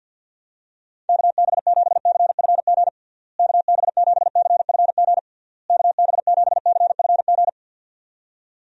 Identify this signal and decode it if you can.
Morse code, heard as a keyed tone, about 48 wpm, 700 Hz. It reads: KB7CFG KB7CFG KB7CFG
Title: USA Call Signs - 6 Characters - Repeated 3x - 50wpm